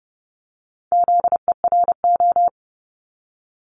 Morse code, heard as a keyed tone, about 30 wpm, 700 Hz. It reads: ZERO
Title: Multi-letter Phonemes 30wpm